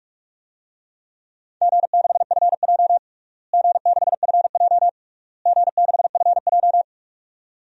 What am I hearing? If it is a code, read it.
Morse code, heard as a keyed tone, about 44 wpm, 700 Hz. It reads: G6FJ G6FJ G6FJ
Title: DXCC-100 Call Signs - 4 Characters - Repeated 3x - 45wpm